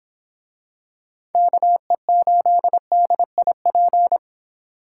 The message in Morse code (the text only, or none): KE8DIP